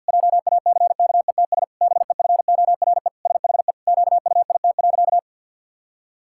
Morse code, heard as a keyed tone, about 50 wpm, 700 Hz. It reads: JACKETS BEFORE SHE QUIT.
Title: Pangram Challenge v2 - 50wpm